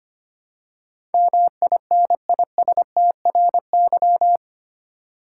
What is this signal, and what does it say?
Morse code, heard as a keyed tone, about 25 wpm, 700 Hz. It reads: MINISTRY
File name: Eight Letter Words - 25wpm